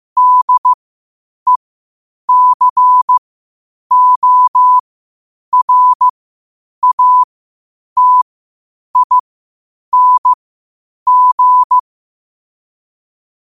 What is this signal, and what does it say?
Morse code, heard as a keyed tone, about 15 wpm, spaced as for 8 wpm, 1000 Hz. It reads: DECORATING
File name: qotdfiles / 8wpm